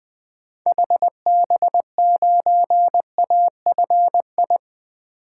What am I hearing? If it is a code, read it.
Morse code, heard as a keyed tone, about 20 wpm, 700 Hz. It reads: HB9AFI
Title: 2023 SOTA Activator Call Signs - 20wpm